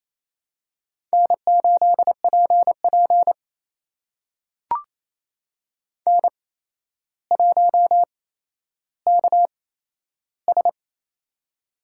Morse code, heard as a keyed tone, about 28 wpm, spaced as for 12 wpm, 700 Hz. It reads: N8PP N1KS
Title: USA Call Signs - 4 Characters - 8x Character Spacing - 28wpm